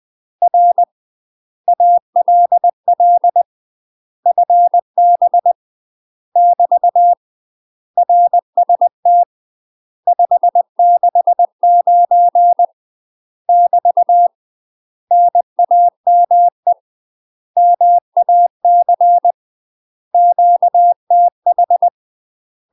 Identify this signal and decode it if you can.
Morse code, heard as a keyed tone, about 20 wpm, 700 Hz. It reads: R ALL FB <BT> RST 569 <BT> NAME MAC QTH
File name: UK MASTER QSOs TRAINING / 20WPM / UK MASTER QSOs 0049